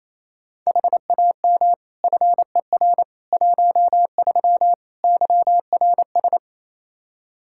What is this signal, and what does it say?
Morse code, heard as a keyed tone, about 28 wpm, 700 Hz. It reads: HAM FER 13 YRS
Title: K1USN SST QSO Practice 28wpm